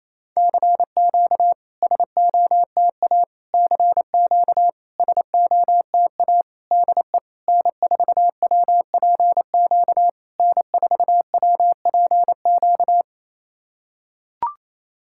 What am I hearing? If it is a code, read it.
Morse code, heard as a keyed tone, about 28 wpm, 700 Hz. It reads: CQ SOTA CQ SOTA DE N4WPQ N4WPQ
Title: SOTA QSO Practice 28wpm